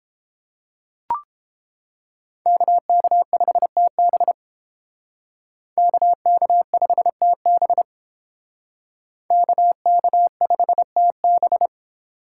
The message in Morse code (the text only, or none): KK5TB KK5TB KK5TB